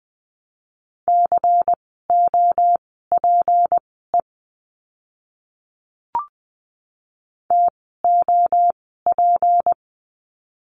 Morse code, heard as a keyed tone, about 20 wpm, spaced as for 15 wpm, 700 Hz. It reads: COPE TOP
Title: 153 Lesson 25f - Letter p - Words 20wpm